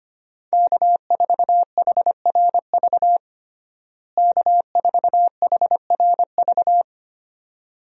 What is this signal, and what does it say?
Morse code, heard as a keyed tone, about 25 wpm, 700 Hz. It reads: K4HRV K4HRV
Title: USA Call Signs - 5 Characters - Repeated 2x - 25wpm